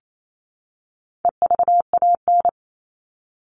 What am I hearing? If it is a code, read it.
Morse code, heard as a keyed tone, about 28 wpm, 700 Hz. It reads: EVAN
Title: All Cut Numbers - 4-Digit 28wpm